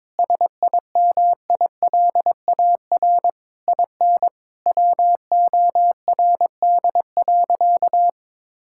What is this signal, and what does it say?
Morse code, heard as a keyed tone, about 22 wpm, 700 Hz. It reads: SIMILAR IN WORD.
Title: Sentences from Top 1000 Words - Review Part 5 - 22wpm